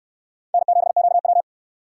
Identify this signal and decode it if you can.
Morse code, heard as a keyed tone, about 60 wpm, 700 Hz. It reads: N7QW